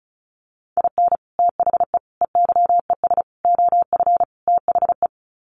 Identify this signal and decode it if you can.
Morse code, heard as a keyed tone, about 34 wpm, 700 Hz. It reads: IN THE EYES OF THE
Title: Sets of 5 Words 35wpm